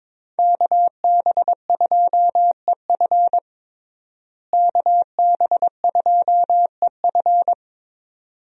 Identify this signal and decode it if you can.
Morse code, heard as a keyed tone, about 22 wpm, 700 Hz. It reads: KB2EF KB2EF
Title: USA Call Signs - 5 Characters - Repeated 2x - 22wpm